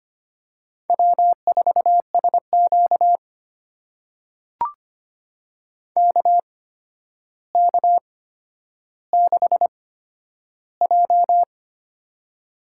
Morse code, heard as a keyed tone, about 25 wpm, spaced as for 11 wpm, 700 Hz. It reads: W4SQ KK6J
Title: USA Call Signs - 4 Characters - 8x Character Spacing - 25wpm